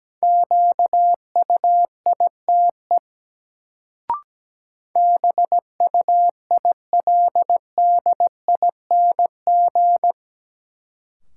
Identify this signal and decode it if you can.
Morse code, heard as a keyed tone, about 17 wpm, 700 Hz. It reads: QUITE BUILDING
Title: Sentences from Top 700 Words - New Word Drill - 17wpm